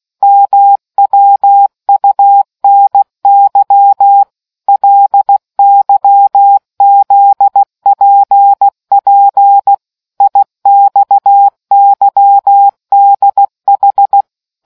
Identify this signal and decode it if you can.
Morse code, heard as a keyed tone, about 16 wpm, 800 Hz. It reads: MWUNY LYZPP IXYDH